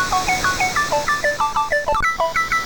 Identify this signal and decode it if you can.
There is no code.